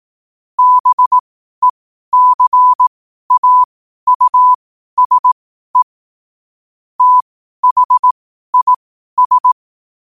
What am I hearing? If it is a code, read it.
Morse code, heard as a keyed tone, about 18 wpm, spaced as for 12 wpm, 1000 Hz. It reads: BECAUSE THIS
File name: qotdfiles / 12wpm